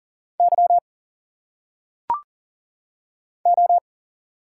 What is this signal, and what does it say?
Morse code, heard as a keyed tone, about 40 wpm, 700 Hz. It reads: Y O
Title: Single Letters - Phonetic Pronunciation 40wpm